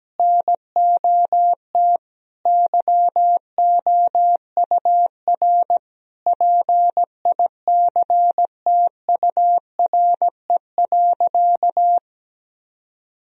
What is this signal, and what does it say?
Morse code, heard as a keyed tone, about 17 wpm, 700 Hz. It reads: NOT YOUR PICTURE.
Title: Sentences from Top 1000 Words - Review Part 1 - 17wpm